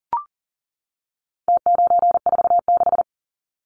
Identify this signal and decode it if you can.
Morse code, heard as a keyed tone, about 40 wpm, 700 Hz. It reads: T946
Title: Numbers - Contest Style - 4-Digit 40wpm